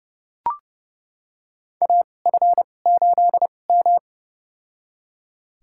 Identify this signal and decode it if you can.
Morse code, heard as a keyed tone, about 30 wpm, spaced as for 22 wpm, 700 Hz. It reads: AF8M
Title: USA Call Signs - 4 Characters - 2x Character Spacing - 30wpm